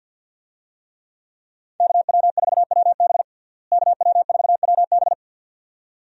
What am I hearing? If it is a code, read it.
Morse code, heard as a keyed tone, about 47 wpm, 700 Hz. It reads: KW4WB KW4WB
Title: USA Call Signs - 5 Characters - Repeated 2x - 50wpm